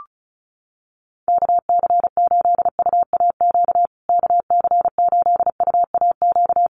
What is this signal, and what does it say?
Morse code, heard as a keyed tone, about 34 wpm, 700 Hz. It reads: KC8UAQ KC8UAQ
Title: USA Call Signs Repeated 3x 35wpm